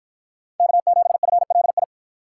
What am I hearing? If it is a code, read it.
Morse code, heard as a keyed tone, about 53 wpm, 700 Hz. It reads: K7FLI